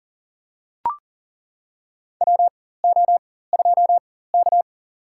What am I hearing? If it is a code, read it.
Morse code, heard as a keyed tone, about 40 wpm, spaced as for 19 wpm, 700 Hz. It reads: WO2K
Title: USA Call Signs - 4 Characters - 4x Character Spacing - 40wpm